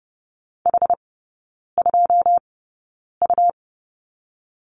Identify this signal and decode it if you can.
Morse code, heard as a keyed tone, about 30 wpm, spaced as for 9 wpm, 690 Hz. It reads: H2U
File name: Lesson 17e - Letter u - Three Characters 30wpm